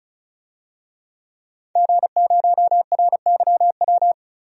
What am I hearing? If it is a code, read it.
Morse code, heard as a keyed tone, about 34 wpm, 700 Hz. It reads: G0RYW
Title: DXCC-100 Call Signs - 5 Characters - 35wpm